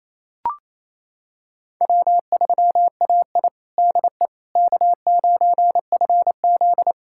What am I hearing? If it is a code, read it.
Morse code, heard as a keyed tone, about 28 wpm, 690 Hz. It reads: W3AI DE K9FZ
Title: USA Callsign DE Callsign - 4 Characters - 28wpm